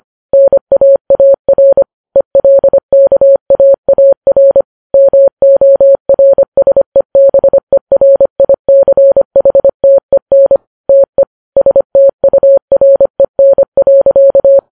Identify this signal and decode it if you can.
Morse code, heard as a keyed tone, about 25 wpm, 550 Hz. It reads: NAAR ELKAAR MORSEBERICHTEN TE STUREN.
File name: Files Web tls / Morse on9fv / morse25kps